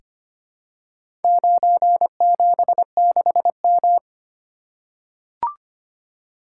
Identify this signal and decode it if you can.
Morse code, heard as a keyed tone, about 25 wpm, 700 Hz. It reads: 976M
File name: practice / 25wpm / Four Letter-Number Combinations - 25wpm